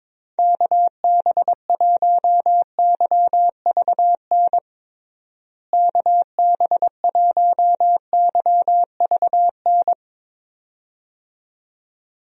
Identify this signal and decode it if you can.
Morse code, heard as a keyed tone, about 22 wpm, 700 Hz. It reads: KB1YVN KB1YVN
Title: USA Call Signs - 6 Characters - Repeated 2x - 22wpm